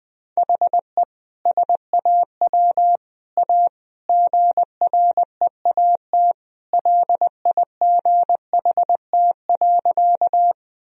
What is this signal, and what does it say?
Morse code, heard as a keyed tone, about 20 wpm, 700 Hz. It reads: HE SAW A GREAT LIGHT.
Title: Sentences from Top 1000 Words - Review Part 1 - 20wpm